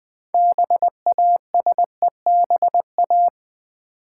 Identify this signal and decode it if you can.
Morse code, heard as a keyed tone, about 20 wpm, 700 Hz. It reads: BASEBA
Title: Build A Word - 2 Letters At A Time - 8-Letter Words - 20wpm